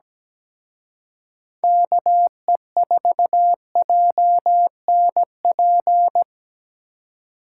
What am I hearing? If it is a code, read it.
Morse code, heard as a keyed tone, about 20 wpm, 700 Hz. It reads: KE4JNP
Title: Speed Racing - USA 2x3 Call Signs 17wpm